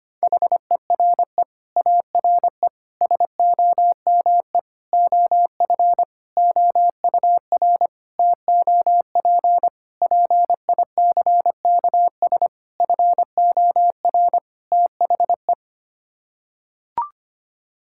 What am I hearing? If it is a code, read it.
Morse code, heard as a keyed tone, about 25 wpm, 700 Hz. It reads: HERE ARE SOME OF OUR TOP PICKS FOR THE
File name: Sets of 9 Words - 25wpm